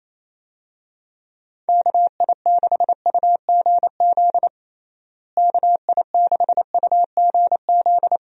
Morse code, heard as a keyed tone, about 28 wpm, 700 Hz. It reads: KI6UGZ KI6UGZ
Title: USA Call Signs - 6 Characters - Repeated 2x - 28wpm